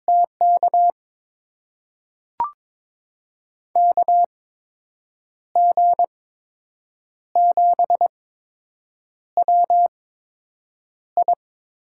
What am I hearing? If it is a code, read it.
Morse code, heard as a keyed tone, about 22 wpm, spaced as for 7 wpm, 700 Hz. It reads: TK KG7WI
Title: USA Call Signs - 6 Characters - 8x Character Spacing - 22wpm